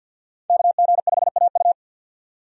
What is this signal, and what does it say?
Morse code, heard as a keyed tone, about 53 wpm, 700 Hz. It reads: KG5AU
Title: USA Call Signs - 5 Characters - 50wpm